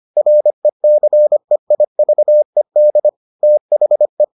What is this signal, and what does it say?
Morse code, heard as a keyed tone, about 25 wpm, 600 Hz. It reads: RECEIVED THE